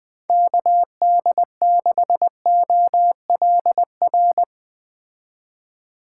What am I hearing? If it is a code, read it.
Morse code, heard as a keyed tone, about 21 wpm, 700 Hz. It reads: KD6OLR